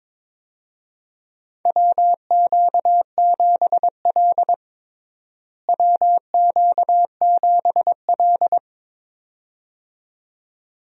Morse code, heard as a keyed tone, about 22 wpm, 700 Hz. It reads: WQ7L WQ7L
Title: USA Call Signs - 4 Characters - Repeated 2x - 22wpm